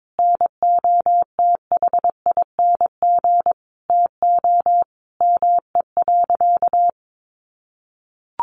Morse code, heard as a keyed tone, about 22 wpm, 700 Hz. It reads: NOTHING TO ME.
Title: Sentences from Top 1400 Words 22wpm